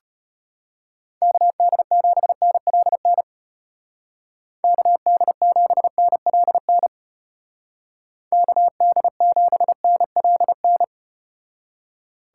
Morse code, heard as a keyed tone, about 33 wpm, 700 Hz. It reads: KD7NLN KD7NLN KD7NLN